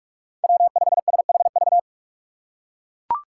Morse code, heard as a keyed tone, about 43 wpm, 700 Hz. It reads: W5SHV
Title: USA Call Signs - 5 Characters - 4x Character Spacing - 45wpm